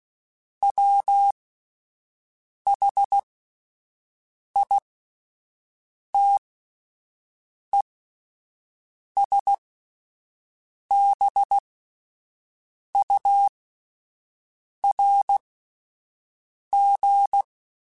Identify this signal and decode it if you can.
Morse code, heard as a keyed tone, about 16 wpm, spaced as for 5 wpm, 800 Hz. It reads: WHITESBURG